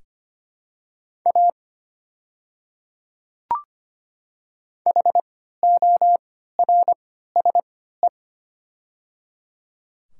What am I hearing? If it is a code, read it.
Morse code, heard as a keyed tone, about 25 wpm, spaced as for 14 wpm, 700 Hz. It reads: A HORSE
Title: 3x Character Spacing - Top 300 Words - 25wpm